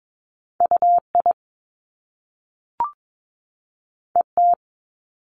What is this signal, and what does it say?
Morse code, heard as a keyed tone, about 22 wpm, 700 Hz. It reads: U I E T
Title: Top Two Letter Combinations in Words 22wpm